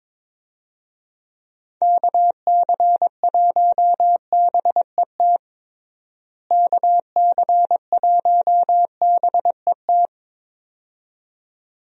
Morse code, heard as a keyed tone, about 22 wpm, 700 Hz. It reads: KC1BET KC1BET
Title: USA Call Signs - 6 Characters - Repeated 2x - 22wpm